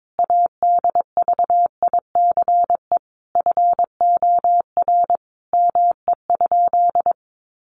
Morse code, heard as a keyed tone, about 22 wpm, 700 Hz. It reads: ADVICE FOR ME?
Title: Sentences from Top 1100 Words 22wpm